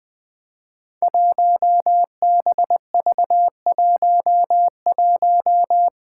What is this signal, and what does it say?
Morse code, heard as a keyed tone, about 20 wpm, 700 Hz. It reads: 1BV11
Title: Code Groups - 5 Letter:Numbers - 20wpm